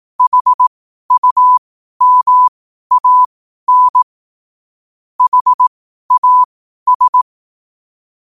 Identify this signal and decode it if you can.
Morse code, heard as a keyed tone, about 18 wpm, spaced as for 12 wpm, 1000 Hz. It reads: HUMAN HAS
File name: qotdfiles / 12wpm